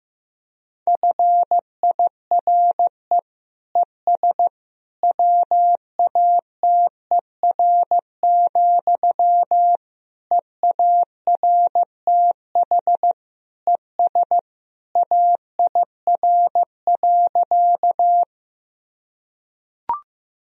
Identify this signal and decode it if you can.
Morse code, heard as a keyed tone, about 15 wpm, 700 Hz. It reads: FIRE ES WATER, EARTH ES AIR.
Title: Sentences from Top 400 Words and Abbreviations 15wpm